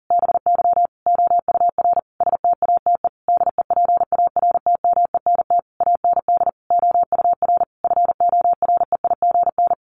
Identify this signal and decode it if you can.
Morse code, heard as a keyed tone, about 40 wpm, 700 Hz. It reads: BY OUR STATE DEPARTMENT AND OUR FOREIGN
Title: Sentences from Top 1400 Words 40wpm